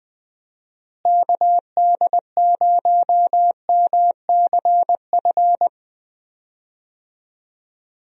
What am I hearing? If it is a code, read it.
Morse code, heard as a keyed tone, about 20 wpm, 700 Hz. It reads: KD0MCF
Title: USA Call Signs - 6 Characters - Repeated 2x - 20wpm